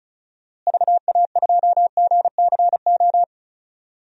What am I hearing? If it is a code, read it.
Morse code, heard as a keyed tone, about 36 wpm, 700 Hz. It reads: VA2GCO
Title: DXCC-100 Call Signs - 6 Characters - Repeated 2x - 35wpm